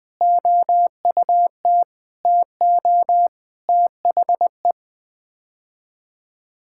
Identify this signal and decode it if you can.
Morse code, heard as a keyed tone, about 20 wpm, 700 Hz. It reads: OUT TO THE
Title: Sets of 8 Words - 20wpm